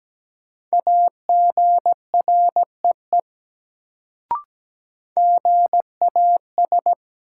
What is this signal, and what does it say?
Morse code, heard as a keyed tone, about 17 wpm, 700 Hz. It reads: AGREE GAS
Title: Sentences from Top 1100 Words - New Word Drill Alt - 17wpm